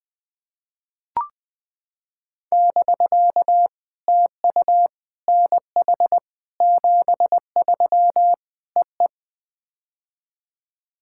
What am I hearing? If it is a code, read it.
Morse code, heard as a keyed tone, about 20 wpm, 700 Hz. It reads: <BK> TU NH 73 EE
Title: POTA QSO Practice - 20wpm